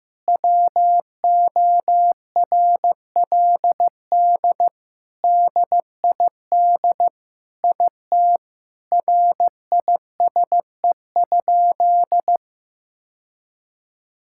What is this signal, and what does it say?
Morse code, heard as a keyed tone, about 15 wpm, 700 Hz. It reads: WORLD DID IT RISE?
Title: Sentences from Top 1000 Words - Review Part 6 - 15wpm